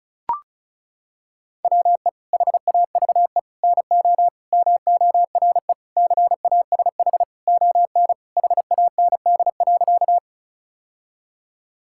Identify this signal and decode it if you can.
Morse code, heard as a keyed tone, about 36 wpm, 700 Hz. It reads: WE HAVE NO MORE CASH ON HAND.
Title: Sentences from Top 1000 Words - Review Part 5 - 35wpm